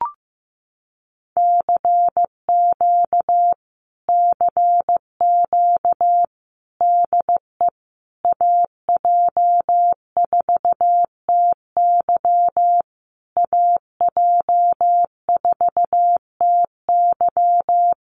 Morse code, heard as a keyed tone, about 15 wpm, 700 Hz. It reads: CQ CQ DE AJ4TY AJ4TY
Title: Common QSO Phrases Encore 15wpm